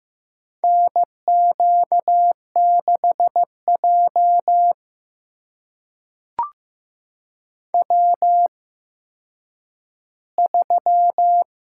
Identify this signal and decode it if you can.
Morse code, heard as a keyed tone, about 15 wpm, spaced as for 9 wpm, 700 Hz. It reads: NQ6J W3
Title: USA Call Signs - 4 Characters - 8x Character Spacing - 15wpm